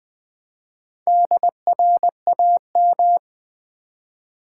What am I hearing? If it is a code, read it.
Morse code, heard as a keyed tone, about 20 wpm, 700 Hz. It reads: DRAM
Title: Build A Word - 4 Letters At A Time - 12-Letter Words - 20wpm